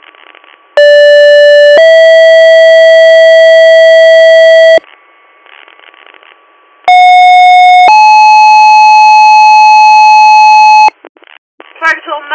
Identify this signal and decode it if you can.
Two-tone page, heard as a one-second tone, then a three-second tone, about 580 then 650 Hz, about 730 then 860 Hz, with speech. It is heard from 0.8 to 10.9 s.